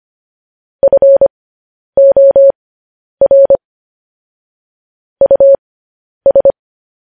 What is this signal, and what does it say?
Morse code, heard as a keyed tone, about 25 wpm, spaced as for 10 wpm, 550 Hz. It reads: FOR US
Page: news headlines in morse code at 10 wpm